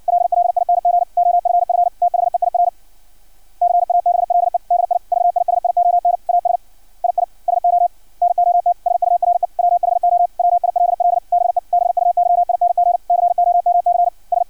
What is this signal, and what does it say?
Morse code, heard as a keyed tone, about 56 wpm, 700 Hz. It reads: LLING OFF THEIR CABLE DIVISION AS II SO NOT SURE WHO WILL BE BUYING COMCS
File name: archived / CRSnet